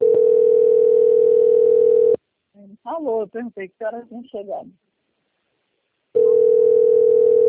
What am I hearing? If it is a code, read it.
Ringback tone (North America), 440 + 480 Hz, with speech.
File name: out- - - - - .52977